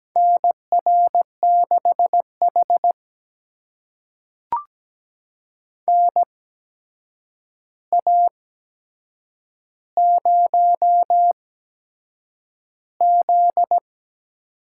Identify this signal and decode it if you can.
Morse code, heard as a keyed tone, about 17 wpm, spaced as for 7 wpm, 700 Hz. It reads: NR6H NA0Z
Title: USA Call Signs - 4 Characters - 8x Character Spacing - 17wpm